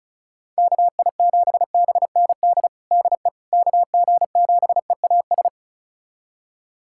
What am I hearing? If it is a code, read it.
Morse code, heard as a keyed tone, about 36 wpm, 700 Hz. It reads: KI7BND DE KG7EAS
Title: USA Callsign DE Callsign - 6 Characters - 35wpm